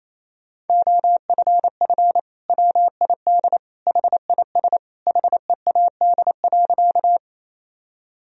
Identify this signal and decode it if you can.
Morse code, heard as a keyed tone, about 28 wpm, 700 Hz. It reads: OFF WID HIS HEAD.
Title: Sentences from Top 300 Words and Abbreviations 28wpm